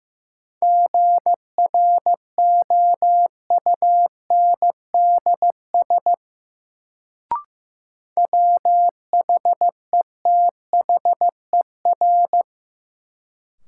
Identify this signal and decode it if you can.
Morse code, heard as a keyed tone, about 16 wpm, 700 Hz. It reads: GROUNDS WHETHER
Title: Sentences from Top 1300 Words - New Word Drill Alt - 15wpm